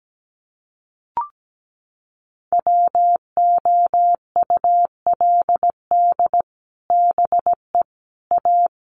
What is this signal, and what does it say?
Morse code, heard as a keyed tone, about 17 wpm, 700 Hz. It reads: WOULD BE A
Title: Sets of 3 Words 17wpm